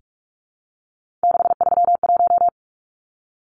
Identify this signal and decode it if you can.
Morse code, heard as a keyed tone, about 44 wpm, 700 Hz. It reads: 631
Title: Numbers - 3-Digit 45wpm